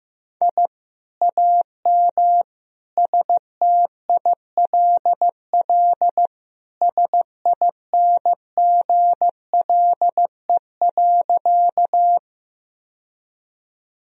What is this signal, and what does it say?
Morse code, heard as a keyed tone, about 16 wpm, 700 Hz. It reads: I AM STILL SINGLE.
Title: Sentences from Top 1000 Words - Review Part 4 - 15wpm